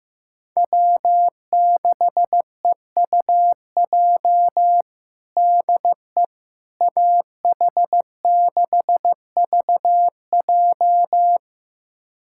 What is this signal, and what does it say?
Morse code, heard as a keyed tone, about 15 wpm, 700 Hz. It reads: W6EUJ DE AH6VJ
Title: USA Callsign DE Callsign - 5 Characters - 15wpm